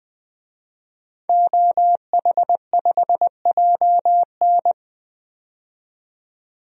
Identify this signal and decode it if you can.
Morse code, heard as a keyed tone, about 20 wpm, 700 Hz. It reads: OH5JN
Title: DXCC-100 Call Signs - 5 Characters - 20wpm